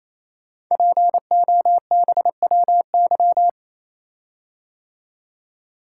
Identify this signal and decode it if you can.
Morse code, heard as a keyed tone, about 29 wpm, 700 Hz. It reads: POBWY
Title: Code Groups - 5 Letters - 28wpm